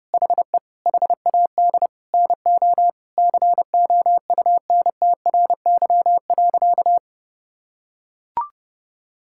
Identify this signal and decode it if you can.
Morse code, heard as a keyed tone, about 30 wpm, 700 Hz. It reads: HE HAD NO COUNTRY.
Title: Sentences from Top 1000 Words - Review Part 5 - 30wpm